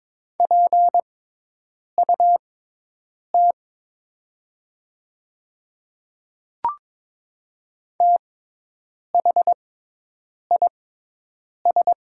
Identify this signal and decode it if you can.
Morse code, heard as a keyed tone, about 22 wpm, spaced as for 8 wpm, 700 Hz. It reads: PUT THIS
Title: 6x Character Spacing - Top 200 Words - 22wpm